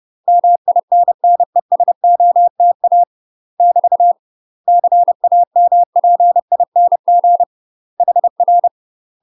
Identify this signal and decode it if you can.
Morse code, heard as a keyed tone, about 30 wpm, 700 Hz. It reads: MINNESOTA <BT> CAMPING HR
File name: UK MASTER QSOs TRAINING / 30WPM / UK MASTER QSOs 0013